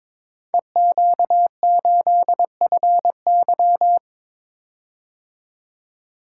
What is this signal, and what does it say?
Morse code, heard as a keyed tone, about 22 wpm, 690 Hz. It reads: EQ8FY
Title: Code Groups - 5 Letter:Numbers - 22wpm